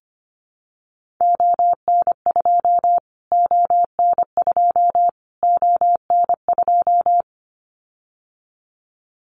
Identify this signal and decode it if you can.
Morse code, heard as a keyed tone, about 25 wpm, 700 Hz. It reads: ON2 ON2 ON2
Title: Callsign Prefixes for Top 25 DXCC - Repeated 3x - 25wpm